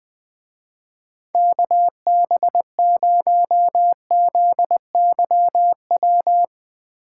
Morse code, heard as a keyed tone, about 19 wpm, 700 Hz. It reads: KB0ZYW
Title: Speed Racing - USA 2x3 Call Signs 17wpm